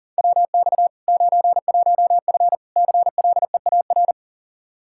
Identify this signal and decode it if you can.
Morse code, heard as a keyed tone, about 40 wpm, 700 Hz. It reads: WX 91F CLEAR
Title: Practice QSO - Exchange 2 - Info - 40wpm